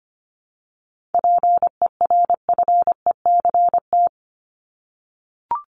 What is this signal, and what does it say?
Morse code, heard as a keyed tone, about 25 wpm, 700 Hz. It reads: PERFECT
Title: Top 751-1000 Words 25wpm